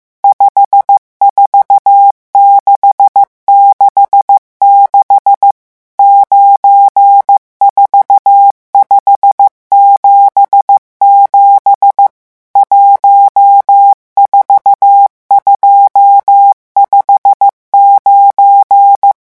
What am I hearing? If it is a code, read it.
Morse code, heard as a keyed tone, about 15 wpm, 800 Hz. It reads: 54666 94577 14259
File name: morse mp3 / revision11 chiffres vitesse 14